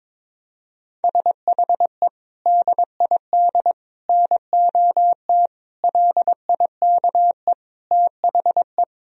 Morse code, heard as a keyed tone, about 22 wpm, 700 Hz. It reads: SHE DID NOT LIKE THE
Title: Sentences from Top 1000 Words - Review Part 3 - 22wpm